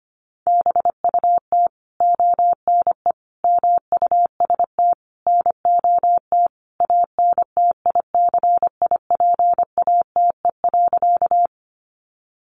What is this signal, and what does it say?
Morse code, heard as a keyed tone, about 25 wpm, 700 Hz. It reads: BUT ONE MUST NOT ANTICIPATE.
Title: Sentences from Top 1500 Words 25wpm